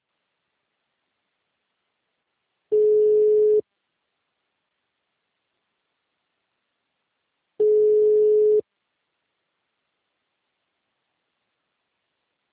Ringback tone (Europe, 425 Hz).